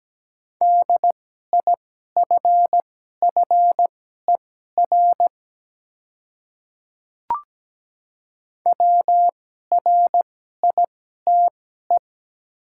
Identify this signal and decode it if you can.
Morse code, heard as a keyed tone, about 17 wpm, spaced as for 12 wpm, 700 Hz. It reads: DIFFER WRITE
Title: 2x Character Spacing - Top 200 Words - 17wpm